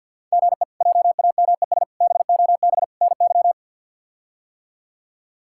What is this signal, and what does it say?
Morse code, heard as a keyed tone, about 52 wpm, 700 Hz. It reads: GE JAMES BOB NY
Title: K1USN SST QSO Practice - 50wpm